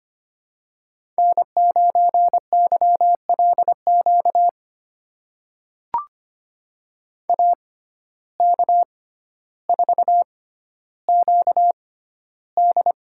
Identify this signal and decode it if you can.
Morse code, heard as a keyed tone, about 25 wpm, spaced as for 13 wpm, 700 Hz. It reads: N9YLQ AK4QD
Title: USA Call Signs - 5 Characters - 6x Character Spacing - 25wpm